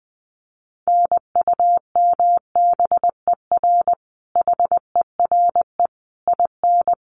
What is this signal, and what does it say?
Morse code, heard as a keyed tone, about 19 wpm, 700 Hz. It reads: NUMBER HERE IN
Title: Sets of 5 Words Encore2 20wpm